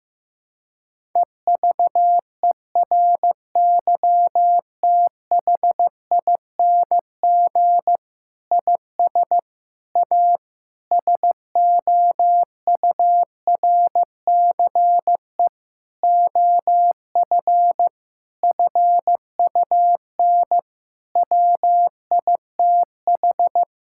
Morse code, heard as a keyed tone, about 15 wpm, 700 Hz. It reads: EVERYTHING IS A SOURCE OF FUN WITH
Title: Sentences from Top 1000 Words - Review Part 5 - 15wpm